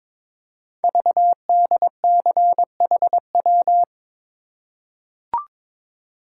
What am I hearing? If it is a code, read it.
Morse code, heard as a keyed tone, about 22 wpm, 690 Hz. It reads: VDCHW
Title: Code Groups - 5 Letters - 22wpm